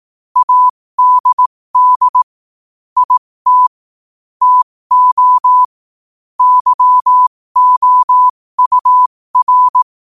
Morse code, heard as a keyed tone, about 18 wpm, spaced as for 15 wpm, 1000 Hz. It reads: ADD IT TO YOUR
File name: qotdfiles / 15wpm